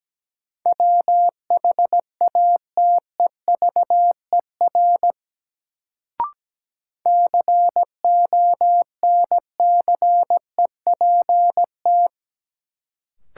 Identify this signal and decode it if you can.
Morse code, heard as a keyed tone, about 17 wpm, 700 Hz. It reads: WHATEVER CONCEPT
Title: Sentences from Top 1100 Words - New Word Drill - 17wpm